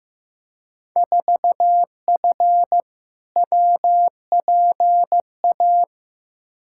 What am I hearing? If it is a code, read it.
Morse code, heard as a keyed tone, about 15 wpm, 700 Hz. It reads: 4F WPA
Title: Field Day Exchanges - 15wpm